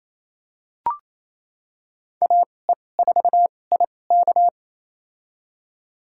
Morse code, heard as a keyed tone, about 28 wpm, spaced as for 20 wpm, 700 Hz. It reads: AE4IK
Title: USA Call Signs - 5 Characters - 2x Character Spacing - 28wpm